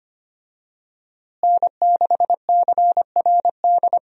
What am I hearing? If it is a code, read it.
Morse code, heard as a keyed tone, about 25 wpm, 700 Hz. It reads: N6CRD
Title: USA Call Signs - 5 Characters - Repeated 3x - 25wpm